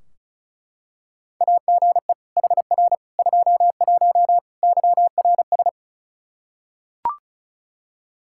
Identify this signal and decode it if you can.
Morse code, heard as a keyed tone, about 36 wpm, 700 Hz. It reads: AGE HR 21 YRS
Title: Practice QSO - Exchange 3 - More Info - 35wpm